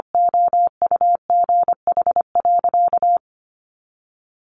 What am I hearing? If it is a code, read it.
Morse code, heard as a keyed tone, about 25 wpm, 700 Hz. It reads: OUGH.
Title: Sentences from Top 400 Words 25wpm